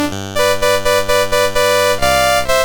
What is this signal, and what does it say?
There is no code.